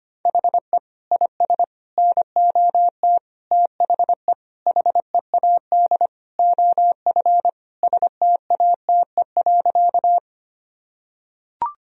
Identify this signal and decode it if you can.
Morse code, heard as a keyed tone, about 25 wpm, 700 Hz. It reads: HE IS NOT THE HEAD OF STATE.
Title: Sentences from Top 1000 Words - Review Part 2 - 25wpm